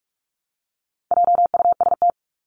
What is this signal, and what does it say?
Morse code, heard as a keyed tone, about 44 wpm, 700 Hz. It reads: JUST
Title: Top 500 Words Repeated 45wpm